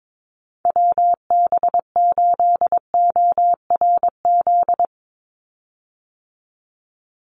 Morse code, heard as a keyed tone, about 22 wpm, 700 Hz. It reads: WB8ORZ
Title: USA Call Signs 22wpm